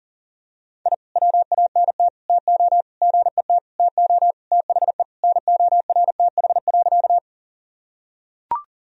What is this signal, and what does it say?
Morse code, heard as a keyed tone, about 40 wpm, 700 Hz. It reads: I WANT TO GET TO THE NORTH.
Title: Sentences from Top 1000 Words - Review Part 1 - 40wpm